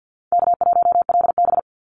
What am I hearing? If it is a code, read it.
Morse code, heard as a keyed tone, about 48 wpm, 700 Hz. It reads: K1LB
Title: NA SOTA Activator Callsigns 50wpm